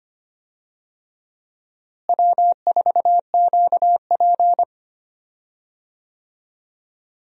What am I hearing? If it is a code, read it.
Morse code, heard as a keyed tone, about 25 wpm, 700 Hz. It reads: W4QP
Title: USA Call Signs - 4 Characters - Repeated 3x - 25wpm